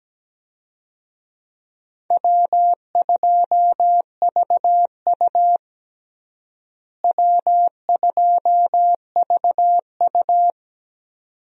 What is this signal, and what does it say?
Morse code, heard as a keyed tone, about 17 wpm, 700 Hz. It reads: W2VU W2VU
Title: USA Call Signs - 4 Characters - Repeated 2x - 17wpm